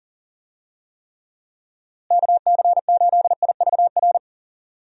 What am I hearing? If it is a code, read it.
Morse code, heard as a keyed tone, about 40 wpm, 700 Hz. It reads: KC8IVR
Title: USA Call Signs - 6 Characters - Repeated 2x - 40wpm